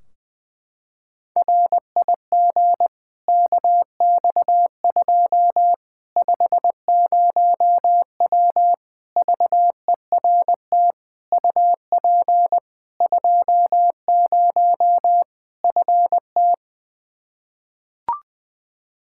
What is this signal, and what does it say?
Morse code, heard as a keyed tone, about 20 wpm, 700 Hz. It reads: RIG KX2 50W VERT UP 20 FT
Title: Practice QSO - Exchange 2 - Info - 20wpm